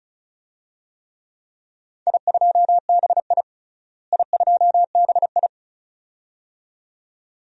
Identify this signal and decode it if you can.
Morse code, heard as a keyed tone, about 36 wpm, 700 Hz. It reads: I2BI I2BI
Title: DXCC-100 Call Signs - 4 Characters - Repeated 2x - 35wpm